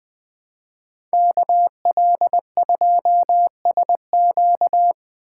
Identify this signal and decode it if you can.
Morse code, heard as a keyed tone, about 19 wpm, 700 Hz. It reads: KL2SQ